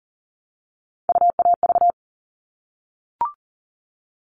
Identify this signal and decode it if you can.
Morse code, heard as a keyed tone, about 40 wpm, 700 Hz. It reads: UAV
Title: All Cut Numbers - 3-Digit 40wpm